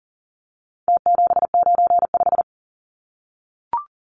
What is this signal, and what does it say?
Morse code, heard as a keyed tone, about 40 wpm, 700 Hz. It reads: T795
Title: Numbers - Contest Style - 4-Digit 40wpm